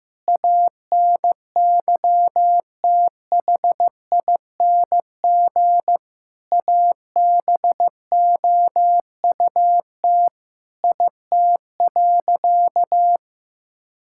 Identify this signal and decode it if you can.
Morse code, heard as a keyed tone, about 15 wpm, 700 Hz. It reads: ANYTHING ABOUT IT.
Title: Sentences from Top 1600 Words 15wpm